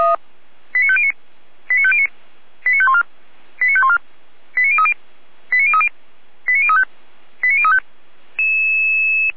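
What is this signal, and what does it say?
Five-tone selective call (ZVEI), heard as quick runs of five tones, tones of about 70 ms. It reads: 89509 89509 89514 89514 89030 89030 89035 89035
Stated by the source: ZVEI five-tone selective call received on SDR, from the feuerwehr-orlamuende.org fire brigade audio server.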